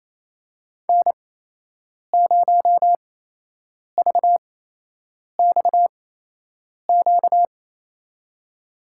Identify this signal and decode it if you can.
Morse code, heard as a keyed tone, about 28 wpm, spaced as for 8 wpm, 700 Hz. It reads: N0VXQ
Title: USA Call Signs - 5 Characters - 8x Character Spacing - 28wpm